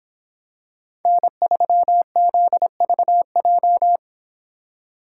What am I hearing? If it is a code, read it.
Morse code, heard as a keyed tone, about 25 wpm, 700 Hz. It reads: N3ZVJ